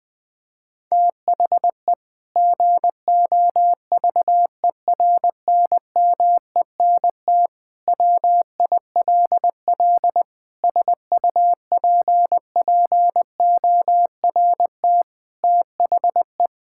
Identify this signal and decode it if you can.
Morse code, heard as a keyed tone, about 20 wpm, 700 Hz. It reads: THE GOVERNMENT WILL SUPPORT THE
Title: Sentences from Top 1000 Words - Review Part 5 - 20wpm